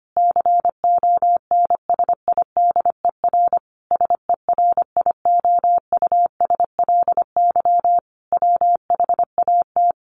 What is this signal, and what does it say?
Morse code, heard as a keyed tone, about 25 wpm, 700 Hz. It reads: CONSIDER SERIOUSLY WHAT
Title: Sentences from Top 1200 Words 25wpm